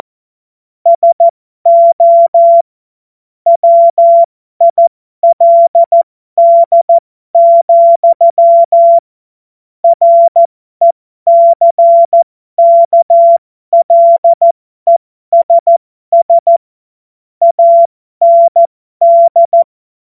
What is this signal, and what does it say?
Morse code, heard as a keyed tone, about 14 wpm, spaced as for 12 wpm, 680 Hz. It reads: SO WILD, RECKLESS AND
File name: Morsekurs / Witze